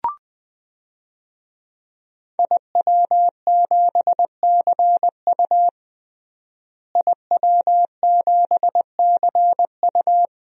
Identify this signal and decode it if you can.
Morse code, heard as a keyed tone, about 20 wpm, 700 Hz. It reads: IW7CU IW7CU
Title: DXCC-100 Call Signs - 5 Characters - Repeated 2x - 20wpm